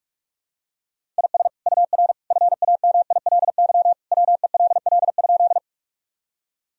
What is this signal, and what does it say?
Morse code, heard as a keyed tone, about 46 wpm, 700 Hz. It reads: IS UR FAMILY WELL?
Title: Sentences from Top 400 Words and Abbreviations 45wpm